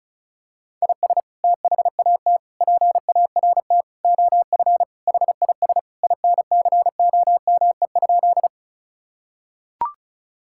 Morse code, heard as a keyed tone, about 36 wpm, 700 Hz. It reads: IS THAT PART OF HIS INCOME?
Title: Sentences from Top 1000 Words - Review Part 5 - 35wpm